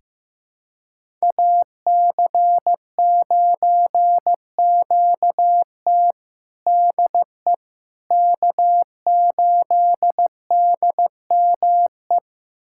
Morse code, heard as a keyed tone, about 16 wpm, 700 Hz. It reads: AC9QT DE K8DME
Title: USA Callsign DE Callsign - 5 Characters - 15wpm